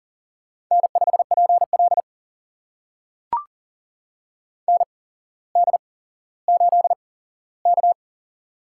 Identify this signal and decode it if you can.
Morse code, heard as a keyed tone, about 40 wpm, spaced as for 17 wpm, 700 Hz. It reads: N5PL ND8K
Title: USA Call Signs - 4 Characters - 8x Character Spacing - 40wpm